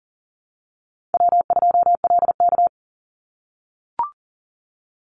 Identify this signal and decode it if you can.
Morse code, heard as a keyed tone, about 41 wpm, 700 Hz. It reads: W2LK
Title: Easy Call Signs - 40wpm